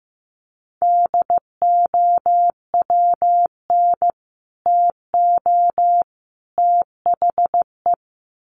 Morse code, heard as a keyed tone, about 15 wpm, 700 Hz. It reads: DOWN TO THE
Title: Sets of 3 Words 15wpm